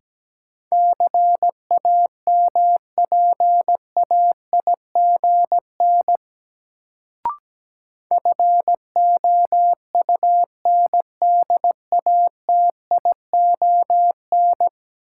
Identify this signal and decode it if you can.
Morse code, heard as a keyed tone, about 17 wpm, 700 Hz. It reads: CAMPAIGN FOUNDATION
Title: Sentences from Top 1200 Words - New Word Drill Alt - 17wpm